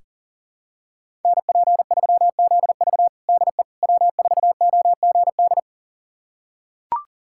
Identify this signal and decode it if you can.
Morse code, heard as a keyed tone, about 40 wpm, 700 Hz. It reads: NP3ZV DE W4OGD
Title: USA Callsign DE Callsign - 5 Characters - 40wpm